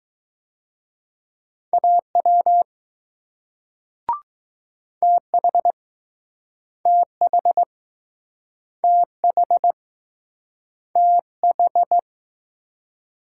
Morse code, heard as a keyed tone, about 20 wpm, 700 Hz. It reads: AW TH TH TH TH